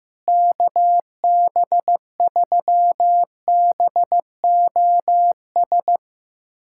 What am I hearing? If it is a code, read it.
Morse code, heard as a keyed tone, about 15 wpm, 700 Hz. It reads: KB3BOS
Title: USA Callsign DE Callsign 15wpm